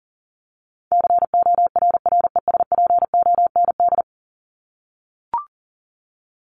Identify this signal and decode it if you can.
Morse code, heard as a keyed tone, about 40 wpm, 700 Hz. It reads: CORRESPOND
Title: ING Suffix - Top 100 Words with Base Words 40wpm